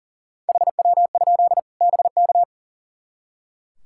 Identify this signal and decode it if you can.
Morse code, heard as a keyed tone, about 40 wpm, 700 Hz. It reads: HW? BK
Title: K1USN SST QSO Practice 40wpm